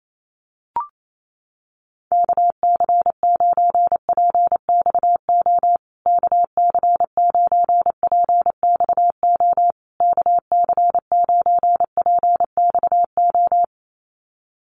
Morse code, heard as a keyed tone, about 28 wpm, 700 Hz. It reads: KC9PXO KC9PXO KC9PXO
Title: USA Call Signs Repeated 3x 28wpm